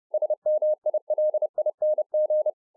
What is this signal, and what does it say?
Morse code, heard as a keyed tone, about 30 wpm, 610 Hz. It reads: SMILING